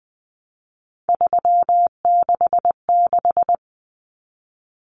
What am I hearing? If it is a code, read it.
Morse code, heard as a keyed tone, about 20 wpm, 700 Hz. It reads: 366
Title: Common Cut Numbers - 3-Digit 20wpm